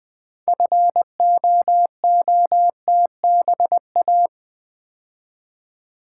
Morse code, heard as a keyed tone, about 20 wpm, 700 Hz. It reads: FOOTBA
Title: Build A Word - 2 Letters At A Time - 8-Letter Words - 20wpm